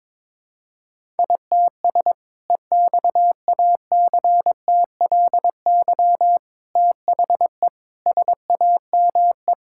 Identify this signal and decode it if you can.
Morse code, heard as a keyed tone, about 22 wpm, 700 Hz. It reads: ITS EXACTLY THE SAME
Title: Sentences from Top 1000 Words - Review Part 6 - 22wpm